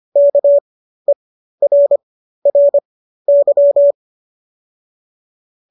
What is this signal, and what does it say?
Morse code, heard as a keyed tone, about 25 wpm, spaced as for 13 wpm, 570 Hz. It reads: KERRY